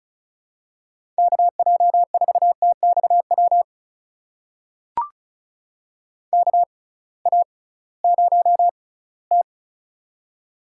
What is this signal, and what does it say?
Morse code, heard as a keyed tone, about 36 wpm, spaced as for 20 wpm, 700 Hz. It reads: KJ4TXW KA0T
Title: USA Call Signs - 6x Character Spacing - 35wpm